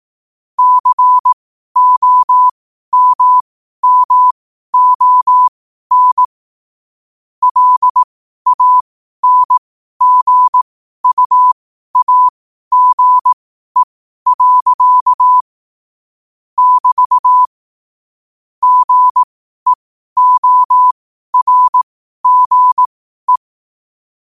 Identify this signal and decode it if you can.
Morse code, heard as a keyed tone, about 18 wpm, spaced as for 12 wpm, 1000 Hz. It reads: COMMON LANGUAGE. <BT> GEORGE
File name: qotdfiles / 12wpm